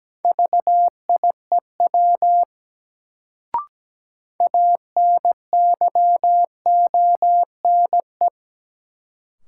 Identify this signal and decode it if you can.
Morse code, heard as a keyed tone, about 17 wpm, 700 Hz. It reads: VIEW ANYONE
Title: Sentences from Top 900 Words - New Word Drill - 17wpm